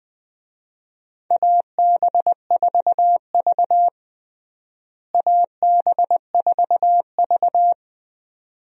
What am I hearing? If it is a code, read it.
Morse code, heard as a keyed tone, about 20 wpm, 700 Hz. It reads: AB4V AB4V
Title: USA Call Signs - 4 Characters - Repeated 2x - 20wpm